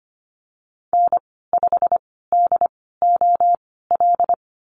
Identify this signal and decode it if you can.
Morse code, heard as a keyed tone, about 25 wpm, spaced as for 16 wpm, 700 Hz. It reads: N5DOL
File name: Lesson 15g - Number 5 - Call Signs 25wpm